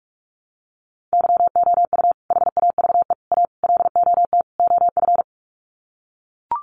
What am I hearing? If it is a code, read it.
Morse code, heard as a keyed tone, about 44 wpm, 700 Hz. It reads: YOU HAVE A LOT OF
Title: Sets of 5 Words Encore 45wpm